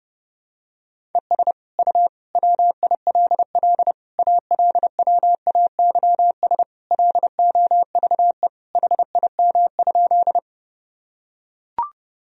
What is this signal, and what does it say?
Morse code, heard as a keyed tone, about 30 wpm, 700 Hz. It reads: ES U WILL ALWAYS LOVE HIM?
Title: Sentences from Top 400 Words and Abbreviations 30wpm